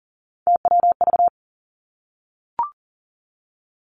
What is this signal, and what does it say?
Morse code, heard as a keyed tone, about 40 wpm, 700 Hz. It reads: TWV
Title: Three Challenging Letter-Number Combinations 40wpm